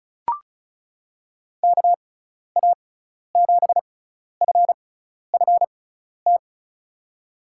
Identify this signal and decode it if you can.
Morse code, heard as a keyed tone, about 36 wpm, spaced as for 12 wpm, 690 Hz. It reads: KA7FFT
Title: USA Call Signs - 6 Characters - 6x Character Spacing - 35wpm